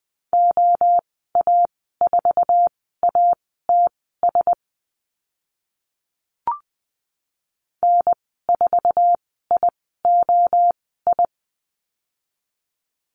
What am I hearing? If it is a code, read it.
Morse code, heard as a keyed tone, about 20 wpm, spaced as for 15 wpm, 700 Hz. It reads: OA4ATS N4IOI
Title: 062 Lesson 11g - Letter h - Call Signs 20wpm